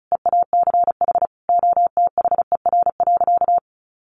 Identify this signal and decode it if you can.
Morse code, heard as a keyed tone, about 34 wpm, 700 Hz. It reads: EACH OTHER.
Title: Sentences from Top 1400 Words 35wpm